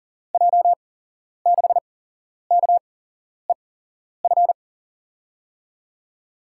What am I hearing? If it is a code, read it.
Morse code, heard as a keyed tone, about 40 wpm, spaced as for 11 wpm, 700 Hz. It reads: J6KEF
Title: USA Call Signs - 6 Characters - 8x Character Spacing - 40wpm